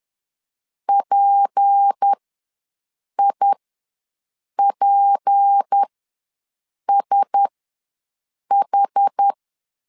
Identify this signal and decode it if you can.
Morse code, heard as a keyed tone, about 10 wpm, spaced as for 6 wpm, 800 Hz. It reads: PIPSH
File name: LEZIONI / 6 PKX0 / 6 P ws 10wpm 3751